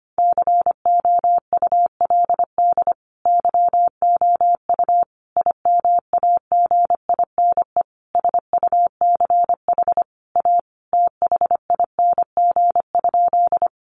Morse code, heard as a keyed tone, about 25 wpm, 700 Hz. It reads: COULD YOU IMAGINE SUCH A THING?
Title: Sentences from Top 1400 Words 25wpm